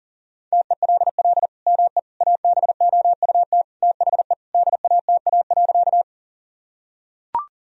Morse code, heard as a keyed tone, about 40 wpm, 700 Hz. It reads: TELL ME ABOUT THE DATA.
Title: Sentences from Top 1000 Words - Review Part 3 - 40wpm